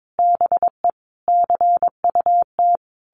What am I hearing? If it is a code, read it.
Morse code, heard as a keyed tone, about 22 wpm, 700 Hz. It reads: BE CUT
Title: A Sound Of Thunder 22wpm